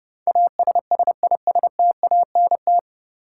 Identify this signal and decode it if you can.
Morse code, heard as a keyed tone, about 31 wpm, 700 Hz. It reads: ASSISTANT
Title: Practice QSO - Exchange 3 - More Info - 30wpm